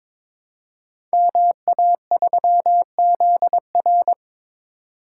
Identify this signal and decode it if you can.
Morse code, heard as a keyed tone, about 22 wpm, 700 Hz. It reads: MA3ZR
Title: Code Groups - 5 Letter:Numbers - 22wpm